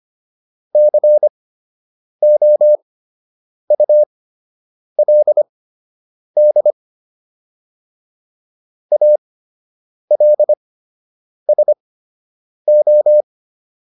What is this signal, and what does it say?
Morse code, heard as a keyed tone, about 25 wpm, spaced as for 8 wpm, 600 Hz. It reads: COULD ALSO